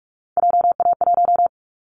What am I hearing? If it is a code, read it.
Morse code, heard as a keyed tone, about 44 wpm, 700 Hz. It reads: JA1
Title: Callsign Prefixes for Top 25 DXCC - Repeated 3x - 45wpm